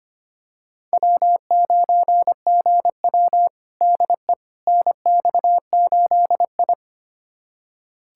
Morse code, heard as a keyed tone, about 25 wpm, 700 Hz. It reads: W9GW DE NX8I
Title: USA Callsign DE Callsign - 4 Characters - 25wpm